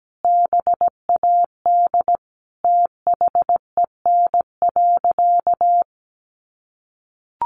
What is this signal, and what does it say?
Morse code, heard as a keyed tone, about 17 wpm, 700 Hz. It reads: BAD THEN.
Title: Sentences from Top 700 Words 17wpm